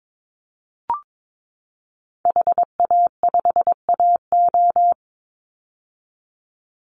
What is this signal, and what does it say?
Morse code, heard as a keyed tone, about 22 wpm, 700 Hz. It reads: HA5AO
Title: Call Signs - 22wpm